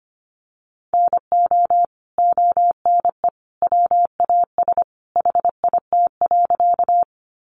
Morse code, heard as a keyed tone, about 25 wpm, 700 Hz. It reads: NO ONE WAS HIT.
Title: Sentences from Top 1200 Words 25wpm